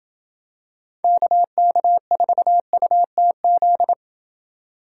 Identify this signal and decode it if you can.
Morse code, heard as a keyed tone, about 27 wpm, 700 Hz. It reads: KK4UTZ